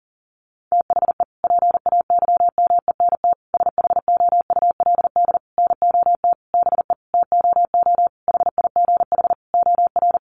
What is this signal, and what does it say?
Morse code, heard as a keyed tone, about 40 wpm, 700 Hz. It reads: THE PAYMENT SHOULD NOT BE TOO HIGH OR
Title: Sentences from Top 1200 Words 40wpm